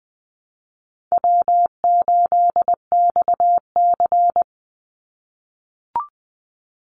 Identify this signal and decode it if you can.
Morse code, heard as a keyed tone, about 20 wpm, 700 Hz. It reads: W8XC
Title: Easy Call Signs - 20wpm